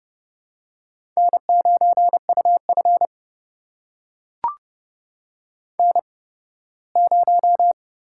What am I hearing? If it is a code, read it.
Morse code, heard as a keyed tone, about 30 wpm, spaced as for 18 wpm, 700 Hz. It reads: N9UF N0
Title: USA Call Signs - 4 Characters - 8x Character Spacing - 30wpm